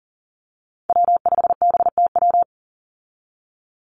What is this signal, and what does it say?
Morse code, heard as a keyed tone, about 41 wpm, 700 Hz. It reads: W5BTW
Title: USA Call Signs 40wpm